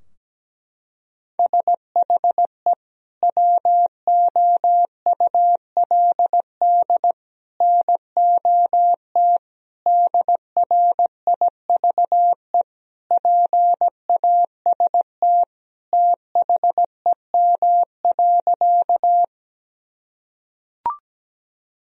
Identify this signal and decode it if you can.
Morse code, heard as a keyed tone, about 17 wpm, 700 Hz. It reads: SHE WOULD NOT DRIVE PAST THEM.
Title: Sentences from Top 1000 Words - Review Part 3 - 17wpm